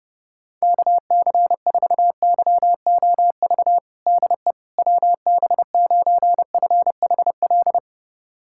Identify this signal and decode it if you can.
Morse code, heard as a keyed tone, about 30 wpm, 700 Hz. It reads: KC4YOV DE WB9FHL
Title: USA Callsign DE Callsign - 6 Characters - 30wpm